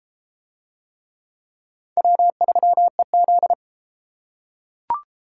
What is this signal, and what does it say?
Morse code, heard as a keyed tone, about 32 wpm, 700 Hz. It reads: W3EZ